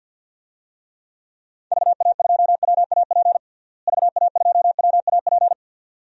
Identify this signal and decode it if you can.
Morse code, heard as a keyed tone, about 52 wpm, 700 Hz. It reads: VA2WAP VA2WAP
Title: DXCC-100 Call Signs - 6 Characters - Repeated 2x - 50wpm